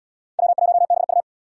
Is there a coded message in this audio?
Yes: Morse code from 0.4 to 1.2 s, heard as a keyed tone, about 71 wpm, 700 Hz. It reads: W1BU